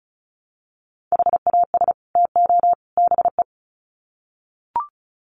Morse code, heard as a keyed tone, about 34 wpm, 700 Hz. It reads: HAS TO BE
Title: Sets of 3 Words 35wpm